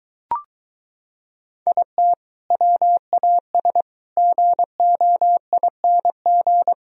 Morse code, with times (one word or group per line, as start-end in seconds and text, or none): 1.7-2.1 IT
2.5-3.8 WAS
4.2-6.7 GOING